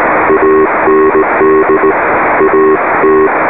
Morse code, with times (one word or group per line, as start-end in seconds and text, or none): 0.3-1.9 AND
2.4-3.3 AT